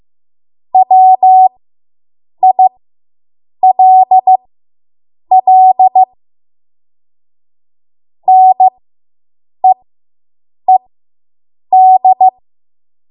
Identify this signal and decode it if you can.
Morse code, heard as a keyed tone, about 15 wpm, spaced as for 7 wpm, 750 Hz. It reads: WILL NEED